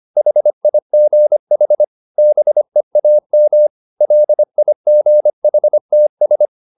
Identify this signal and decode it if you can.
Morse code, heard as a keyed tone, about 25 wpm, 600 Hz. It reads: HIGH BEAM LIGHTS